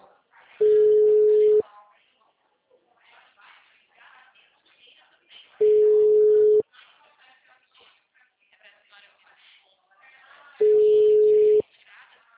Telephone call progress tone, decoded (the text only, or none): ringback tone